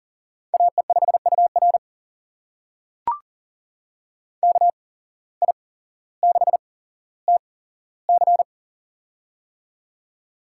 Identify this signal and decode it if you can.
Morse code, heard as a keyed tone, about 40 wpm, spaced as for 17 wpm, 700 Hz. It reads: AE5UR KI6TC
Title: USA Call Signs - 5 Characters - 8x Character Spacing - 40wpm